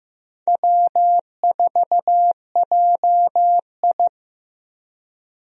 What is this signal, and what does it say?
Morse code, heard as a keyed tone, about 15 wpm, 700 Hz. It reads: W4JI
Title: USA Call Signs - 4 Characters - 15wpm